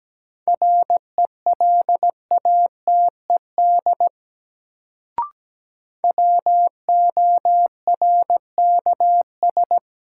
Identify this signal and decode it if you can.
Morse code, heard as a keyed tone, about 17 wpm, 700 Hz. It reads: RELATED WORKS
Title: Sentences from Top 1000 Words - Review Part 5 - New Word Drill - 17wpm